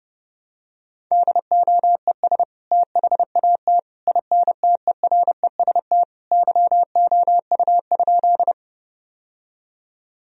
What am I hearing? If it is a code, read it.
Morse code, heard as a keyed tone, about 30 wpm, 700 Hz. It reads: DOES THAT INTEREST YOU?
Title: Sentences from Top 1000 Words - Review Part 2 - 30wpm